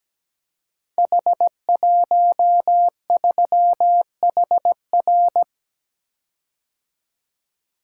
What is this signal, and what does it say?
Morse code, heard as a keyed tone, about 17 wpm, 700 Hz. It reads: H13HR
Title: Code Groups - 5 Letter:Numbers - 17wpm